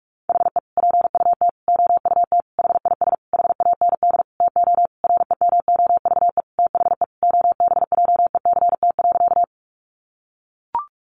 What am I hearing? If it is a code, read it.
Morse code, heard as a keyed tone, about 44 wpm, 700 Hz. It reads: HE PUT OUT HIS HAND TO REMOVE THE OBJECT.
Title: Sentences from Top 800 Words 45wpm